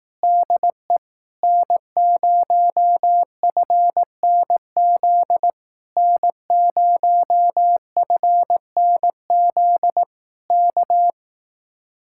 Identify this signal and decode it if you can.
Morse code, heard as a keyed tone, about 18 wpm, 700 Hz. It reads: DE N0FNZ N0FNZ K
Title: Speed-Racing Common QSO Phrases Encore - 15wpm